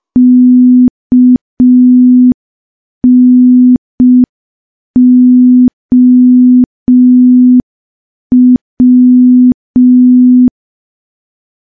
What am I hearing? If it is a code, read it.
Morse code, heard as a keyed tone, about 5 wpm, 250 Hz. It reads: KNOW
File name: morse / morse-training-05WPM